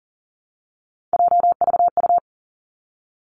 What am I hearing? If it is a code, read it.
Morse code, heard as a keyed tone, about 40 wpm, 700 Hz. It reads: JVU
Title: Three Challenging Letter-Number Combinations 40wpm